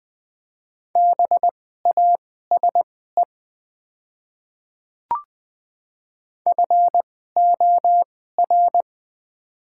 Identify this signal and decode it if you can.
Morse code, heard as a keyed tone, about 20 wpm, spaced as for 14 wpm, 700 Hz. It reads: BASE FOR
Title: 2x Character Spacing - Top 300 Words - 20wpm